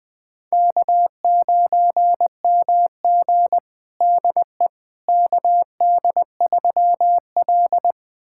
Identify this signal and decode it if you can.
Morse code, heard as a keyed tone, about 21 wpm, 700 Hz. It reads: K9MG DE KD3L
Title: USA Callsign DE Callsign - 4 Characters - 20wpm